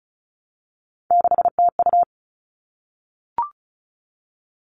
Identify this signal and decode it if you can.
Morse code, heard as a keyed tone, about 34 wpm, 700 Hz. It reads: 6TU
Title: All Cut Numbers - 3-Digit 35wpm